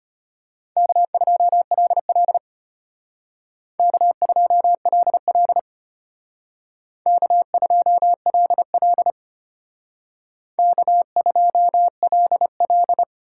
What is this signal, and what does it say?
Morse code, heard as a keyed tone, about 31 wpm, 700 Hz. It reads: K2LL K2LL K2LL K2LL
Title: Speed Racing - USA 1x2 Call Signs 25wpm